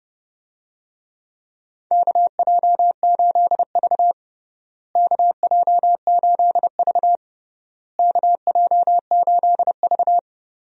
Morse code, heard as a keyed tone, about 32 wpm, 700 Hz. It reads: KJ8V KJ8V KJ8V
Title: USA Call Signs - 4 Characters - Repeated 3x - 30wpm